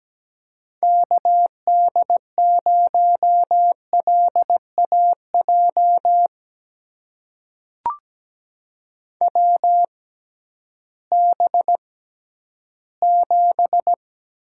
Morse code, heard as a keyed tone, about 17 wpm, spaced as for 11 wpm, 700 Hz. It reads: KD0LAJ WB7
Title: USA Call Signs - 6 Characters - 6x Character Spacing - 17wpm